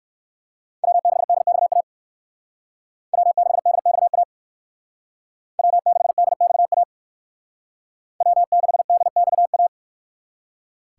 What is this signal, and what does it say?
Morse code, heard as a keyed tone, about 59 wpm, 700 Hz. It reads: W6DXA W6DXA W6DXA W6DXA